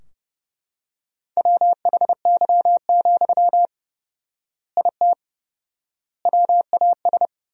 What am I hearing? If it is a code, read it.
Morse code, heard as a keyed tone, about 30 wpm, 700 Hz. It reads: WHY, IT WAS
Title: 4x Word-Spacing - Sentences from Top 300 Words - 30wpm